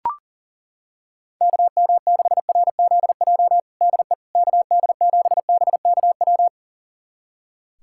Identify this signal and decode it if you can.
Morse code, heard as a keyed tone, about 40 wpm, 700 Hz. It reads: KM6RZJ DE KD7BKW
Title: USA Callsign DE Callsign - 6 Characters - 40wpm